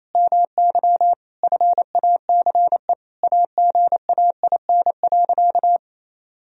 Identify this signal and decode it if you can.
Morse code, heard as a keyed tone, about 28 wpm, 700 Hz. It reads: MY FACE AGAIN.
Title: Sentences from Top 1000 Words - Review Part 2 - 28wpm